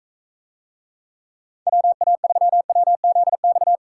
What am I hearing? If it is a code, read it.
Morse code, heard as a keyed tone, about 42 wpm, 700 Hz. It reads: WA3WZX